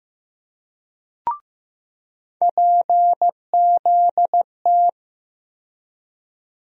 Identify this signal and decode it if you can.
Morse code, heard as a keyed tone, about 15 wpm, 700 Hz. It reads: PZT
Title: USA Call Sign Postfixes - 3 Letters 15wpm